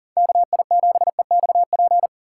Morse code, heard as a keyed tone, about 53 wpm, 700 Hz. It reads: KI7EXP